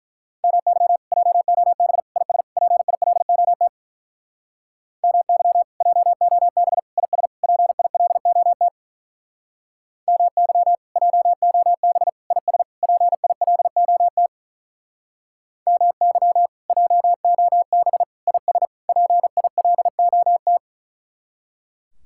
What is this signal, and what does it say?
Morse code, heard as a keyed tone, about 47 wpm, 700 Hz. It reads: MY JOB IS PILOT MY JOB IS PILOT MY JOB IS PILOT MY JOB IS PILOT